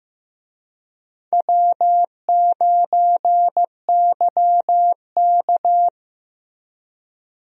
Morse code, heard as a keyed tone, about 15 wpm, 700 Hz. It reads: W9YK
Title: USA Call Signs - 4 Characters - Repeated 3x - 15wpm